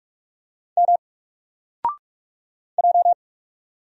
Morse code, heard as a keyed tone, about 46 wpm, 700 Hz. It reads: M J
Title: Single Letter-Number 45wpm